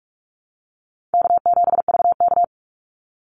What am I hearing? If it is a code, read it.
Morse code, heard as a keyed tone, about 44 wpm, 700 Hz. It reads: K7VK
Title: NA SOTA Chaser Callsigns 45wpm